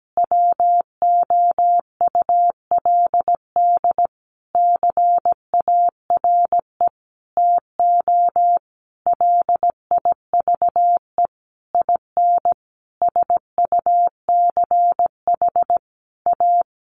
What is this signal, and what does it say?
Morse code, heard as a keyed tone, about 17 wpm, 700 Hz. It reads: WOULD CARE TO LIVE IN SUCH A
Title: Sentences from Top 1500 Words 17wpm